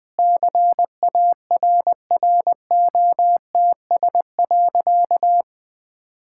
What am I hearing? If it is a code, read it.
Morse code, heard as a keyed tone, about 21 wpm, 700 Hz. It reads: CARROTS.
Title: Easy Sentences with Hints - 20wpm